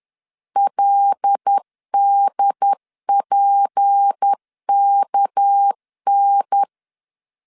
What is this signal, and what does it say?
Morse code, heard as a keyed tone, about 10 wpm, 800 Hz. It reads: LDPKN
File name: LEZIONI / 6 PKX0 / 6 PKX 10wpm 2426